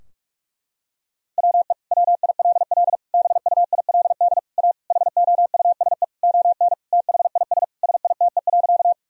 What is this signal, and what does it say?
Morse code, heard as a keyed tone, about 41 wpm, 700 Hz. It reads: WE WILL BUILD A HOUSE ON THIS SITE.
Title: Sentences from Top 1000 Words - Review Part 4 - 45wpm